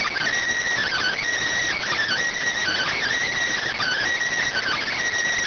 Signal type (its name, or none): Emergency Alert System (SAME) tones